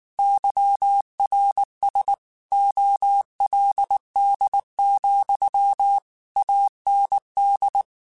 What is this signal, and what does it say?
Morse code, heard as a keyed tone, about 19 wpm, 800 Hz. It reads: YRS OLD, AND
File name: QSOs / QSOs-mp3s / kc7sac@18